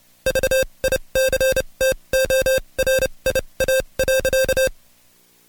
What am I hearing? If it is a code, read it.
Morse code, heard as a keyed tone, about 29 wpm, spaced as for 23 wpm, 530 Hz. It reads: VICTORIA.
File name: Code Quick / AudioDownld2 / Seg138-Stepper11 23WPM